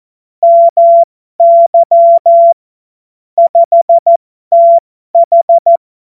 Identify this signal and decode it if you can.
Morse code, heard as a keyed tone, about 14 wpm, spaced as for 12 wpm, 680 Hz. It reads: MY 5TH
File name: Morsekurs / Witze